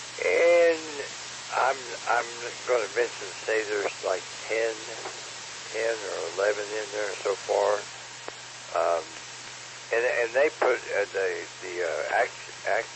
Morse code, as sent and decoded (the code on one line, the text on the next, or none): none